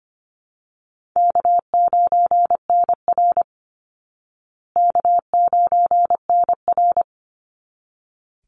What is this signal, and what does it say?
Morse code, heard as a keyed tone, about 25 wpm, 700 Hz. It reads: K9NR K9NR
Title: Easy Call Signs Repeated 2x - 25wpm